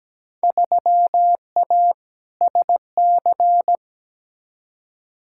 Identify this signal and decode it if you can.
Morse code, heard as a keyed tone, about 17 wpm, 700 Hz. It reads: 3A SC
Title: Field Day Exchanges Encore - 17wpm